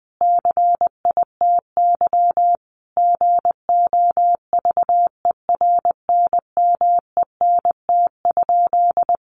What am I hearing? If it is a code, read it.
Morse code, heard as a keyed tone, about 20 wpm, 700 Hz. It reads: CITY GOVERNMENT?
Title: Sentences from Top 1400 Words 20wpm